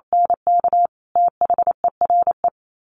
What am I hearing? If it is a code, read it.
Morse code, heard as a keyed tone, about 28 wpm, 700 Hz. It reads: NK THERE
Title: Sets of 5 Words 28wpm